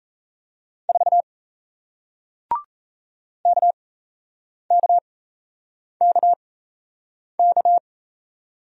Morse code, heard as a keyed tone, about 37 wpm, spaced as for 8 wpm, 700 Hz. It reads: 4 KKKK